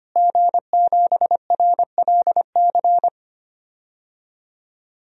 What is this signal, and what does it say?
Morse code, heard as a keyed tone, about 25 wpm, 700 Hz. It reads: G7RLC
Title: DXCC-100 Call Signs - 5 Characters - Repeated 2x - 25wpm